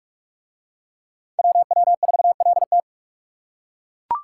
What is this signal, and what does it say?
Morse code, heard as a keyed tone, about 40 wpm, 700 Hz. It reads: WW4LT